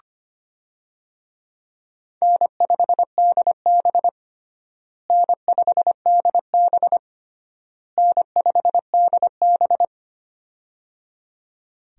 Morse code, heard as a keyed tone, about 25 wpm, 700 Hz. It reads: N5DB N5DB N5DB
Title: USA Call Signs - 4 Characters - Repeated 3x - 25wpm